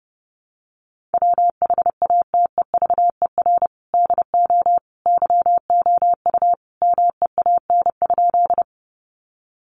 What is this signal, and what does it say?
Morse code, heard as a keyed tone, about 30 wpm, 700 Hz. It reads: WHATEVER DO YOU MEAN?
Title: Sentences from Top 1100 Words 30wpm